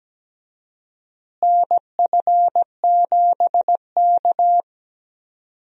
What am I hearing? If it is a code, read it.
Morse code, heard as a keyed tone, about 17 wpm, 700 Hz. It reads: NF7K
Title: USA Call Signs - 4 Characters - Repeated 2x - 17wpm